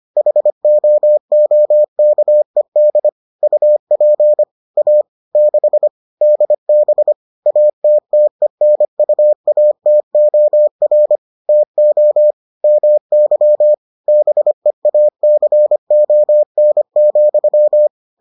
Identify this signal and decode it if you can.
Morse code, heard as a keyed tone, about 25 wpm, 600 Hz. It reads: HOOKED UP A 6 DB ATTENUATOR TO MY BEACON,